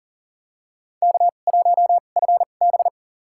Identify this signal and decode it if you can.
Morse code, heard as a keyed tone, about 41 wpm, spaced as for 29 wpm, 700 Hz. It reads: K1FB
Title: USA Call Signs - 4 Characters - 2x Character Spacing - 40wpm